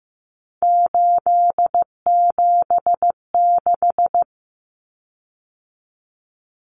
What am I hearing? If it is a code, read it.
Morse code, heard as a keyed tone, about 15 wpm, 700 Hz. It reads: 876
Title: Common Cut Numbers - 3-Digit 15wpm